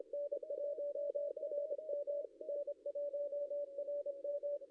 Morse code, heard as a keyed tone, about 25 wpm, 560 Hz. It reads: N1RM N1RM
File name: contests / ss-cw-k5tr-op-n5rz -0700